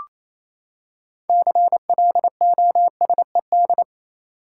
Morse code, heard as a keyed tone, about 28 wpm, 700 Hz. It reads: CLOSED
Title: Six Letter Words - 28wpm